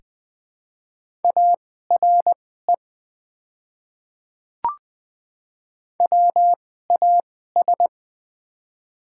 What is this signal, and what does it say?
Morse code, heard as a keyed tone, about 20 wpm, spaced as for 15 wpm, 700 Hz. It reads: ARE WAS
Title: 2x Character Spacing - Top 100 Words - 20wpm